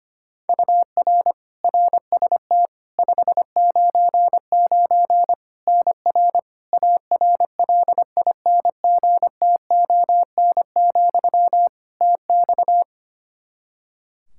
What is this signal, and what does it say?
Morse code, heard as a keyed tone, about 25 wpm, 700 Hz. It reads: UR RST 599 NR ARLINGTON, TX
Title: Practice QSO - Exchange 1 - Beginning - 25wpm